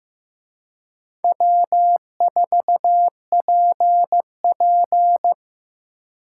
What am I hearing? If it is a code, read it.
Morse code, heard as a keyed tone, about 18 wpm, 700 Hz. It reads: W4PP
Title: Speed Racing - USA 1x2 Call Signs 15wpm